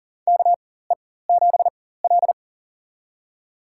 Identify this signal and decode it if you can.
Morse code, heard as a keyed tone, about 40 wpm, spaced as for 19 wpm, 700 Hz. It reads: KE7L
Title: USA Call Signs - 4x Character Spacing - 40wpm